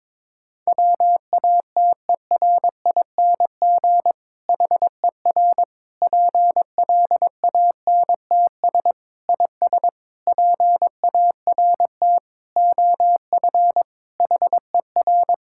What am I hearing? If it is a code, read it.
Morse code, heard as a keyed tone, about 22 wpm, 700 Hz. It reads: WATERING HER PLANTS IS PART OF HER
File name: Difficult Sentences with Hints - 22wpm